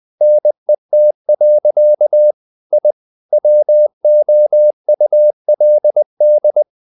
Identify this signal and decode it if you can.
Morse code, heard as a keyed tone, about 20 wpm, 600 Hz. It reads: NET. I WOULD
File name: the old man and the sea / the old man and the sea part 1 20wpm